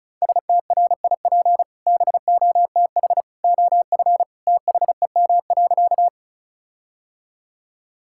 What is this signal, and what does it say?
Morse code, heard as a keyed tone, about 34 wpm, 700 Hz. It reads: STRIP BOTH OF THEM.
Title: Sentences from Top 1600 Words 35wpm